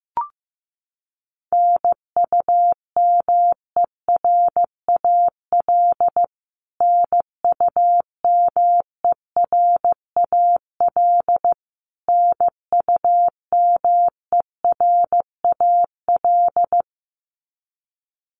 Morse code, heard as a keyed tone, about 15 wpm, 700 Hz. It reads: NUMERAL NUMERAL NUMERAL
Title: Top 500 Words Repeated 15wpm